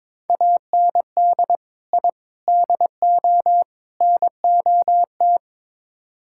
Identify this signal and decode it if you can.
Morse code, heard as a keyed tone, about 22 wpm, 700 Hz. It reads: AND I DO NOT
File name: practice / 22wpm / Build A Phrase - 5-Word Phrases - 22wpm